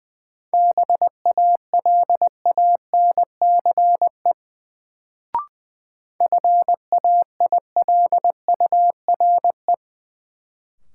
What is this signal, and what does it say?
Morse code, heard as a keyed tone, about 20 wpm, 700 Hz. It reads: BALANCE FAILURE
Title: Sentences from Top 900 Words - New Word Drill - 20wpm